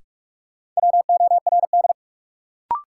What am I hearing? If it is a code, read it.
Morse code, heard as a keyed tone, about 46 wpm, 700 Hz. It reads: WORD
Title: Sentences from Top 300 Words - New Word Drill - 45wpm